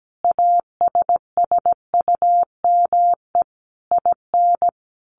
Morse code, heard as a keyed tone, about 17 wpm, 700 Hz. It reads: ASSUME IN
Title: Sentences from Top 1500 Words 17wpm